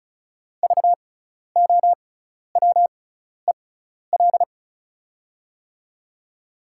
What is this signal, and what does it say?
Morse code, heard as a keyed tone, about 36 wpm, spaced as for 12 wpm, 690 Hz. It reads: VOWEL
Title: 6x Character Spacing - Top 500 Words - 35wpm